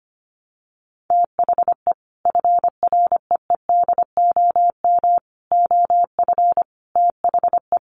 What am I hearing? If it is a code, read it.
Morse code, heard as a keyed tone, about 25 wpm, 700 Hz. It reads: THE FREEDOM OF THE
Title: Sentences from Top 800 Words 25wpm